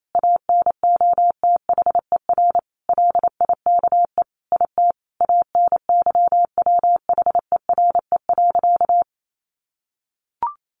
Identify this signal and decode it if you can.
Morse code, heard as a keyed tone, about 28 wpm, 700 Hz. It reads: ANOTHER LIKE IT ANYWHERE.
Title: Sentences from Top 1400 Words 28wpm